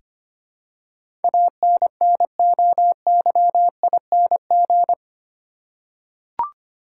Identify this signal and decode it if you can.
Morse code, heard as a keyed tone, about 25 wpm, 700 Hz. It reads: ANNOYING
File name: Eight Letter Words - 25wpm